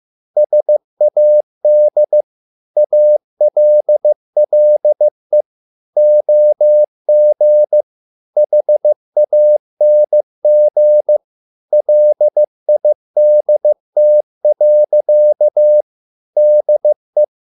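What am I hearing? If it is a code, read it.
Morse code, heard as a keyed tone, about 15 wpm, 600 Hz. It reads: SAD ALLE OG HANG LIDT. DE
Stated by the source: Morse code at a slow 15 wpm from CW med Gnister.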